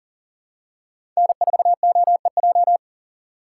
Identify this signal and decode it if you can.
Morse code, heard as a keyed tone, about 38 wpm, 700 Hz. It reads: N4OEJ
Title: USA Call Signs - 5 Characters - 3x Character Spacing - 40wpm